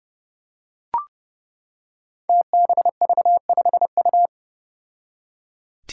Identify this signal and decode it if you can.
Morse code, heard as a keyed tone, about 30 wpm, 700 Hz. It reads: TBV5U
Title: Five Challenging Letter-Number Combinations 30wpm